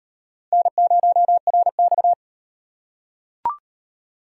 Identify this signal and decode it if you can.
Morse code, heard as a keyed tone, about 38 wpm, 700 Hz. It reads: N0RX